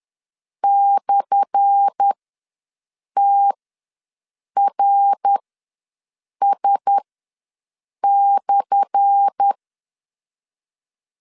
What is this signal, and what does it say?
Morse code, heard as a keyed tone, about 10 wpm, spaced as for 6 wpm, 800 Hz. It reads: /TRS/
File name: LEZIONI / 10 SIGNS / 10A SLASH ws 10wpm 4121